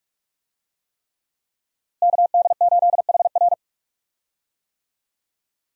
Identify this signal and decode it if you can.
Morse code, heard as a keyed tone, about 42 wpm, 700 Hz. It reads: KD8HR
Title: USA Call Signs - 5 Characters - Repeated 3x - 45wpm